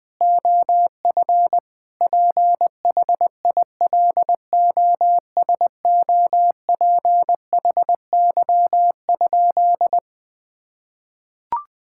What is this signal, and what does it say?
Morse code, heard as a keyed tone, about 21 wpm, 700 Hz. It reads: OF PHILOSOPHY?
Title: Sentences from Top 1000 Words - Review Part 6 - 20wpm